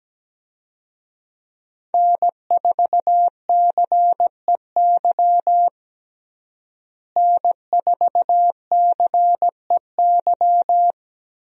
Morse code, heard as a keyed tone, about 17 wpm, 700 Hz. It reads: N4CEY N4CEY
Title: USA Call Signs - 5 Characters - Repeated 2x - 17wpm